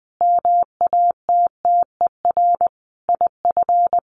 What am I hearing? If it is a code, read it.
Morse code, heard as a keyed tone, about 20 wpm, 700 Hz. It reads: MATTER IF
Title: Sets of 5 Words Encore 20wpm